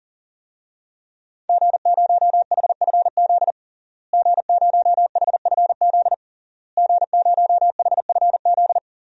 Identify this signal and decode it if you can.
Morse code, heard as a keyed tone, about 40 wpm, 700 Hz. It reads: G0HFZ G0HFZ G0HFZ
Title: DXCC-100 Call Signs - 5 Characters - Repeated 3x - 40wpm